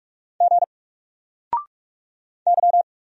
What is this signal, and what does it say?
Morse code, heard as a keyed tone, about 44 wpm, 700 Hz. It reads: G Y
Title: Single Letter-Number 45wpm